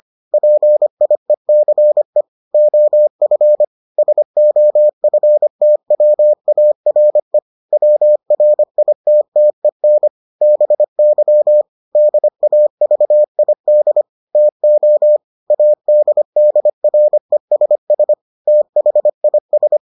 Morse code, heard as a keyed tone, about 25 wpm, 600 Hz. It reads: PIECE OF SOFTWARE WRITTEN BY DAVID TO ADDRESS THIS